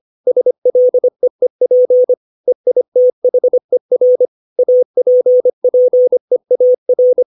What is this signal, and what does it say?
Morse code, heard as a keyed tone, about 25 wpm, 500 Hz. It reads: SLEEP EITHER APPEAR
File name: 632 words / 632 Words 25WPM